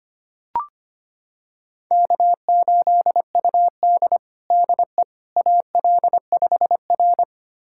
Morse code, heard as a keyed tone, about 25 wpm, 700 Hz. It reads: K8UD DE AL5R
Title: USA Callsign DE Callsign - 4 Characters - 25wpm